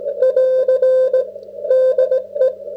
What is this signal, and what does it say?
Morse code, heard as a keyed tone, about 23 wpm, spaced as for 19 wpm, 530 Hz. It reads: <AR> DE